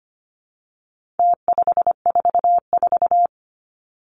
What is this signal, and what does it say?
Morse code, heard as a keyed tone, about 25 wpm, 700 Hz. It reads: T544
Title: Numbers - Contest Style - 4-Digit 25wpm